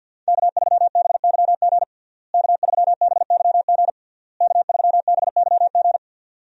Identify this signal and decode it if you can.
Morse code, heard as a keyed tone, about 48 wpm, 700 Hz. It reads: K3BYG K3BYG K3BYG
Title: USA Call Signs - 5 Characters - Repeated 3x - 50wpm